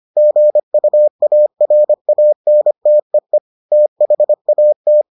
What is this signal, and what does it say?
Morse code, heard as a keyed tone, about 25 wpm, 600 Hz. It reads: GUARANTEE THAT